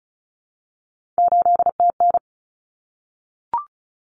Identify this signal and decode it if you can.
Morse code, heard as a keyed tone, about 34 wpm, 700 Hz. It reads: 8TN
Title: Common Cut Numbers - 3-Digit 35wpm